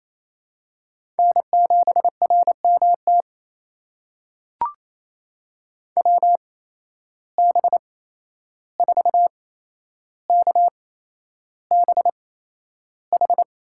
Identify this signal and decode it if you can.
Morse code, heard as a keyed tone, about 28 wpm, spaced as for 11 wpm, 700 Hz. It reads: N7RMT WB4KBH
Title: USA Call Signs - 8x Character Spacing - 28wpm